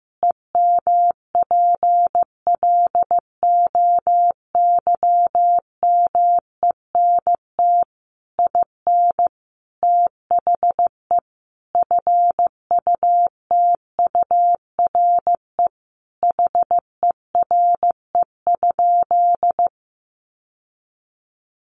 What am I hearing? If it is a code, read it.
Morse code, heard as a keyed tone, about 15 wpm, 700 Hz. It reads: EMPLOYMENT IN THE FUTURE HERE?
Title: Sentences from Top 1500 Words 15wpm